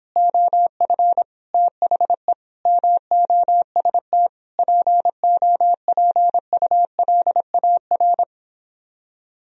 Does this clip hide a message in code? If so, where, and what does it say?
Yes: Morse code from 0.2 to 8.2 s, heard as a keyed tone, about 31 wpm, 700 Hz. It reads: OF THE MOST POPULAR